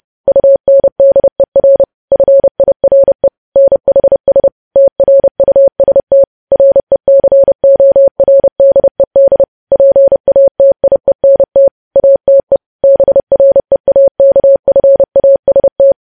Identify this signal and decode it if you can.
Morse code, heard as a keyed tone, about 30 wpm, 550 Hz. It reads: UNDER FIRE NHS TRUST RECORDED PATIENT ATE BREAKFAST
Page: news headlines in morse code at 30 wpm